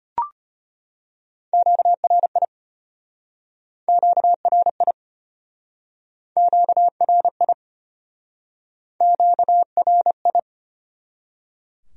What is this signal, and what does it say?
Morse code, heard as a keyed tone, about 36 wpm, 700 Hz. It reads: QRI QRI QRI QRI